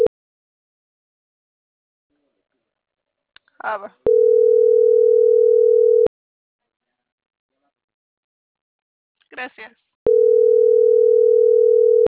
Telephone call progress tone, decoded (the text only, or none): ringback tone